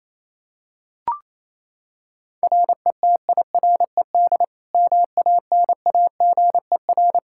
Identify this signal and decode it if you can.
Morse code, heard as a keyed tone, about 27 wpm, 700 Hz. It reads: RETIRED MANAGER
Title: Practice QSO - Exchange 3 - More Info - 28wpm